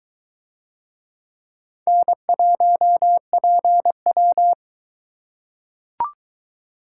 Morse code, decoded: N1PW

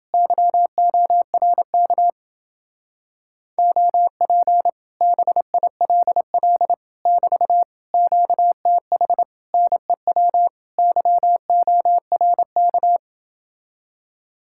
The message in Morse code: YORK OP BILL <BT> QTH NEW YORK

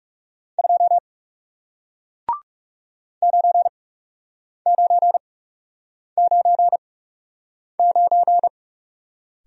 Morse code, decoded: 2 9999

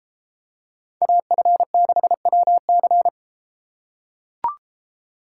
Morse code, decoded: AF6WC

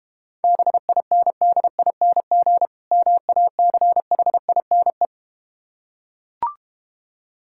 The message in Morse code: BINDING MACHINE

